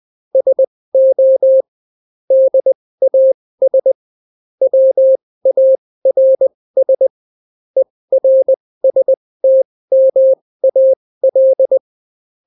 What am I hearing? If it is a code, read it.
Morse code, heard as a keyed tone, about 20 wpm, spaced as for 16 wpm, 530 Hz. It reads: SO DAS WARS ERSTMAL